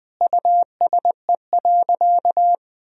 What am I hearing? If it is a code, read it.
Morse code, heard as a keyed tone, about 21 wpm, 700 Hz. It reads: USE.
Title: Sentences from Top 1000 Words - Review Part 4 - 20wpm